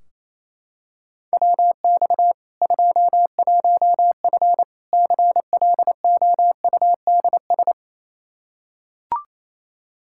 Morse code, heard as a keyed tone, about 28 wpm, 700 Hz. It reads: WX 21F CLOUDS
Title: K1USN SST QSO Practice 28wpm